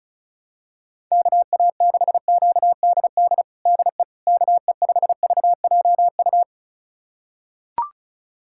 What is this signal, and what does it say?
Morse code, heard as a keyed tone, about 36 wpm, 700 Hz. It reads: KA6QDD DE KE5VJU
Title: USA Callsign DE Callsign 35wpm